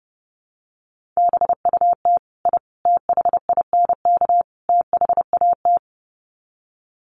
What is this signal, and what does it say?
Morse code, heard as a keyed tone, about 30 wpm, 700 Hz. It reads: BUT I THINK THAT
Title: Sets of 4 Words 30wpm